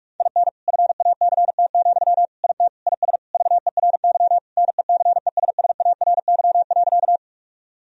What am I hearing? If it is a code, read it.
Morse code, heard as a keyed tone, about 46 wpm, 700 Hz. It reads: IN FACT, IT IS VERY NECESSARY.
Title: Sentences from Top 1000 Words - Review Part 3 - 45wpm